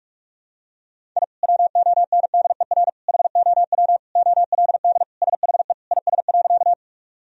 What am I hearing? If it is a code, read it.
Morse code, heard as a keyed tone, about 46 wpm, 700 Hz. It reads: I WONDER HOW OLD SHE IS.
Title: Sentences from Top 1000 Words - Review Part 3 - 45wpm